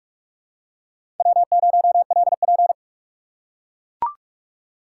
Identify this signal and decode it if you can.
Morse code, heard as a keyed tone, about 43 wpm, 700 Hz. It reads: W0LP
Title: USA Call Signs - 4 Characters - 4x Character Spacing - 45wpm